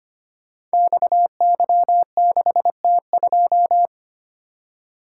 Morse code, heard as a keyed tone, about 25 wpm, 700 Hz. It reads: XY6T2
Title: Code Groups - 5 Letter:Numbers - 25wpm